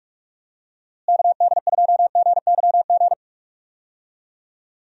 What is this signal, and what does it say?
Morse code, heard as a keyed tone, about 46 wpm, 700 Hz. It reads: KD2GYG
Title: USA Call Signs - 6 Characters - Repeated 2x - 45wpm